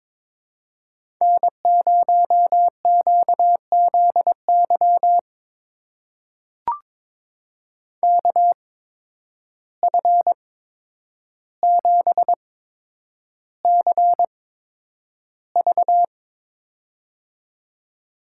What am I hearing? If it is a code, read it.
Morse code, heard as a keyed tone, about 23 wpm, spaced as for 10 wpm, 700 Hz. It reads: N0QZY KF7CV
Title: USA Call Signs - 5 Characters - 8x Character Spacing - 22wpm